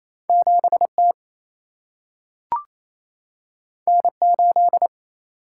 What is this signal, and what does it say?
Morse code, heard as a keyed tone, about 28 wpm, 700 Hz. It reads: 7T N8
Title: Single-Letter Single-Number 28wpm